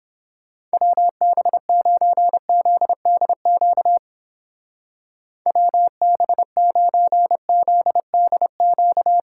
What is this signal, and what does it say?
Morse code, heard as a keyed tone, about 29 wpm, 700 Hz. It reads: WB9ZDQ WB9ZDQ